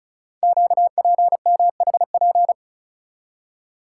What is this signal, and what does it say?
Morse code, heard as a keyed tone, about 36 wpm, 690 Hz. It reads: QPMHP
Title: Code Groups - 5 Letters - 35wpm